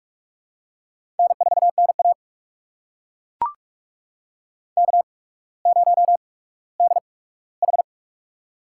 Morse code, heard as a keyed tone, about 43 wpm, spaced as for 19 wpm, 700 Hz. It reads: N4NA K0DH
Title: USA Call Signs - 4 Characters - 8x Character Spacing - 45wpm